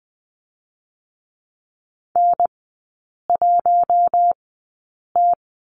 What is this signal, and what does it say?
Morse code, heard as a keyed tone, about 20 wpm, spaced as for 8 wpm, 700 Hz. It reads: N1T
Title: 081 Lesson 14e - Number 2 - Three Characters 20wpm